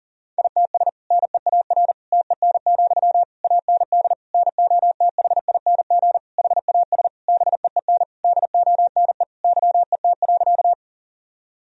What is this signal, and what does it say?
Morse code, heard as a keyed tone, about 40 wpm, 700 Hz. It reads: ITS NEAR TEN, AND NOTHING HAS BEEN DONE YET.
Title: Sentences from Top 1000 Words - Review Part 2 - 40wpm